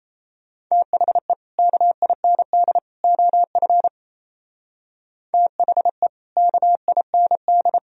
Morse code, heard as a keyed tone, about 35 wpm, 700 Hz. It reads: THE KIND OF THE KIND
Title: Speed-Racing - Sets of 3 Words 28wpm